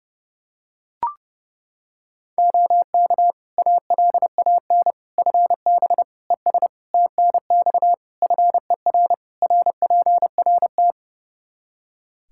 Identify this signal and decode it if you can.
Morse code, heard as a keyed tone, about 32 wpm, 700 Hz. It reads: OK ALAN FB ES TNX FER RPRT
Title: Practice QSO - Exchange 2 - Info - 30wpm